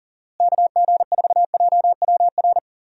Morse code, heard as a keyed tone, about 40 wpm, 700 Hz. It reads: KG4JWR
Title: USA Call Signs - 6 Characters - Repeated 2x - 40wpm